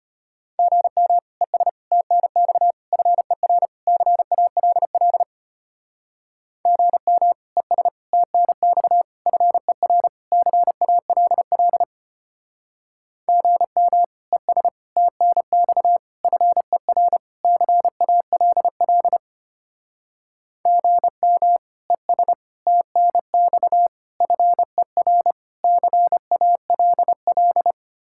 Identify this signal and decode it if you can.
Morse code, heard as a keyed tone, about 32 wpm, 700 Hz. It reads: GM ES TNX FER CALL GM ES TNX FER CALL GM ES TNX FER CALL GM ES TNX FER CALL